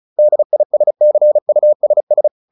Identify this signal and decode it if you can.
Morse code, heard as a keyed tone, about 34 wpm, 600 Hz. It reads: DISCUSS